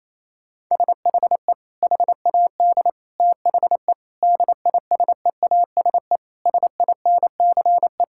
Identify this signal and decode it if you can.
Morse code, heard as a keyed tone, about 28 wpm, 700 Hz. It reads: SHE HAD THE DISEASE SINCE
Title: Sentences from Top 1000 Words - Review Part 6 - 28wpm